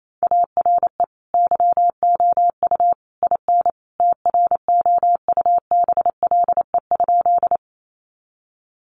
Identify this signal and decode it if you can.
Morse code, heard as a keyed tone, about 28 wpm, 700 Hz. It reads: ARE YOU IN TROUBLE?
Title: Sentences from Top 1200 Words 28wpm